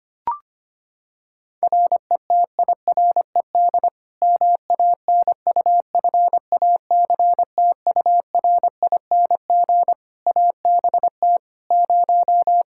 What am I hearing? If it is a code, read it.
Morse code, heard as a keyed tone, about 25 wpm, 700 Hz. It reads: RETIRED MANUFACTURING ABT 0
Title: Practice QSO - Exchange 3 - More Info - 25wpm